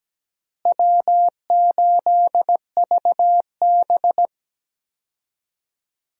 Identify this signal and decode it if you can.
Morse code, heard as a keyed tone, about 17 wpm, 700 Hz. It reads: W8VB
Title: USA Call Signs - 4 Characters - Repeated 3x - 17wpm